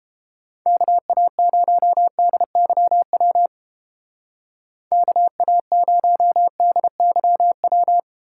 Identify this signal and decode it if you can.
Morse code, heard as a keyed tone, about 31 wpm, 700 Hz. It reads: KA0DYW KA0DYW